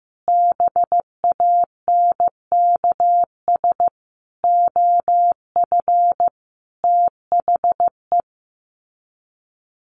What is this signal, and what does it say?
Morse code, heard as a keyed tone, about 15 wpm, 700 Hz. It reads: BANKS OF THE
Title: Sets of 5 Words Encore2 15wpm